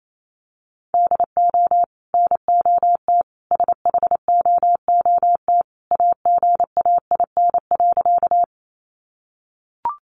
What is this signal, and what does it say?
Morse code, heard as a keyed tone, about 28 wpm, 700 Hz. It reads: DO NOT SHOOT AGAIN.
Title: Sentences from Top 1200 Words 28wpm